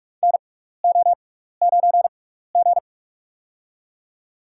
Morse code, heard as a keyed tone, about 43 wpm, spaced as for 15 wpm, 700 Hz. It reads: NO9G
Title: USA Call Signs - 4 Characters - 6x Character Spacing - 45wpm